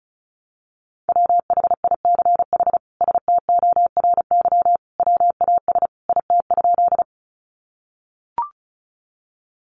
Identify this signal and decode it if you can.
Morse code, heard as a keyed tone, about 34 wpm, 700 Hz. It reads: WHICH STORY WAS IT?
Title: Sentences from Top 300 Words 35wpm